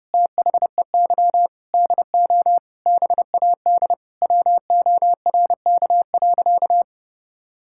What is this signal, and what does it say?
Morse code, heard as a keyed tone, about 30 wpm, 700 Hz. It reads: THEY DO BAD WORK.
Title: Sentences from Top 1000 Words - Review Part 6 - 30wpm